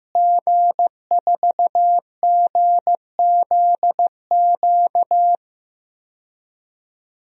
Morse code, heard as a keyed tone, about 15 wpm, 700 Hz. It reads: G4GZQ
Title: DXCC-100 Call Signs - 5 Characters - Repeated 3x - 15wpm